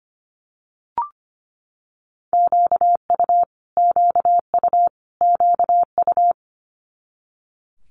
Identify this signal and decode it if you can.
Morse code, heard as a keyed tone, about 25 wpm, 700 Hz. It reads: QU QU QU
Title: Multi-letter Phonemes Repeated 3x 25wpm